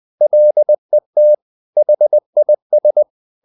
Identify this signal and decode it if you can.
Morse code, heard as a keyed tone, about 20 wpm, 600 Hz. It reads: LET HIS
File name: the old man and the sea part 2 20wpm